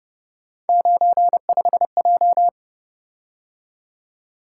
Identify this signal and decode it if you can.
Morse code, heard as a keyed tone, about 29 wpm, 700 Hz. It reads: 95J
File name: Speed-Racing - Three Letter-Number Combinations - 25wpm